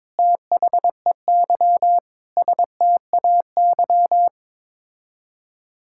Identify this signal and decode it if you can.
Morse code, heard as a keyed tone, about 21 wpm, 700 Hz. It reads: THEY STAY
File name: practice / 22wpm / Sets of 8 Words - 22wpm